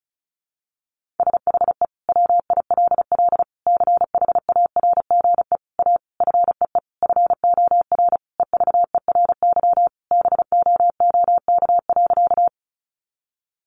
Morse code, heard as a keyed tone, about 34 wpm, 700 Hz. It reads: SHE WILL CHARGE A FEE FOR EVERY BOOK.
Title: Sentences from Top 1500 Words 35wpm